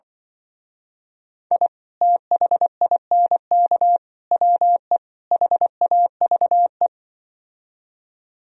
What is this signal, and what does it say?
Morse code, heard as a keyed tone, about 24 wpm, 700 Hz. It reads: I THINK WE HAVE
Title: Speed-Racing - Sets of 4 Words 20wpm